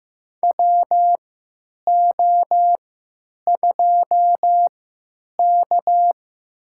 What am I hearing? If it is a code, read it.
Morse code, heard as a keyed tone, about 15 wpm, spaced as for 9 wpm, 700 Hz. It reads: WO2K
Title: USA Call Signs - 4 Characters - 3x Character Spacing - 15wpm